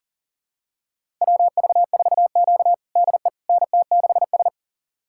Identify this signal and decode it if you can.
Morse code, heard as a keyed tone, about 40 wpm, 700 Hz. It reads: WV4Q DE NT6S
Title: USA Callsign DE Callsign - 4 Characters - 40wpm